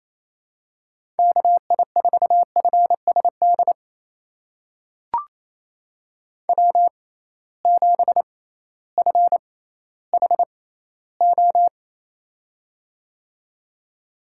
Morse code, heard as a keyed tone, about 28 wpm, spaced as for 15 wpm, 700 Hz. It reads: KI4FSD W7FHO
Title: USA Call Signs - 6x Character Spacing - 28wpm